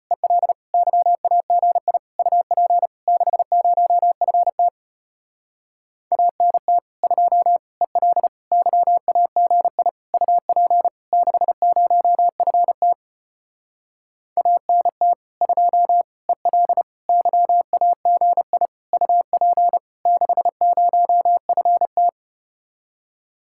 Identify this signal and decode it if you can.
Morse code, heard as a keyed tone, about 33 wpm, 700 Hz. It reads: EL YAGI UP 60FT ANT 2 EL YAGI UP 60FT ANT 2 EL YAGI UP 60FT